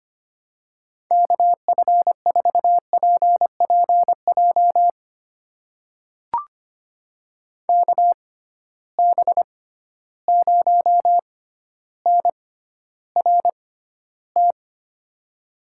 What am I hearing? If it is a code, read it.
Morse code, heard as a keyed tone, about 25 wpm, spaced as for 13 wpm, 700 Hz. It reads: KF4PPJ KB0NRT
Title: USA Call Signs - 6 Characters - 6x Character Spacing - 25wpm